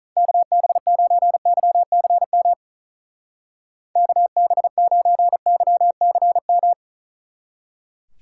Morse code, heard as a keyed tone, about 45 wpm, 700 Hz. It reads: KB9YCM KB9YCM